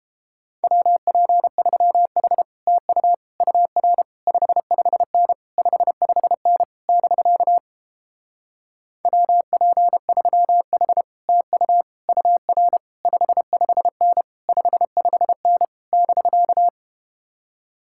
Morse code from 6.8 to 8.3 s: <BK>

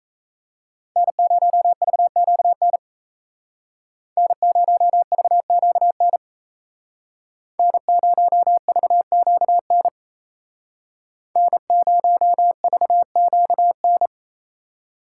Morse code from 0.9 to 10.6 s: N0VQN N0VQN N0VQN

